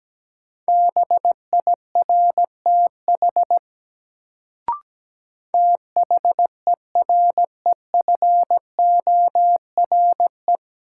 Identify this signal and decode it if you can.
Morse code, heard as a keyed tone, about 17 wpm, 700 Hz. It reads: BIRTH THEREFORE
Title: Sentences from Top 800 Words - New Word Drill - 17wpm